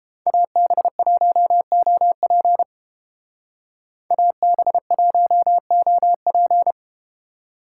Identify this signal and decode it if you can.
Morse code, heard as a keyed tone, about 30 wpm, 700 Hz. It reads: AB1OP AB1OP